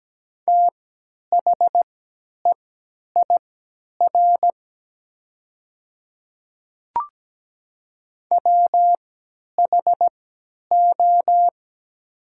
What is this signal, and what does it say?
Morse code, heard as a keyed tone, about 17 wpm, spaced as for 10 wpm, 700 Hz. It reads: THEIR WHO
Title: 3x Character Spacing - Top 100 Words - 17wpm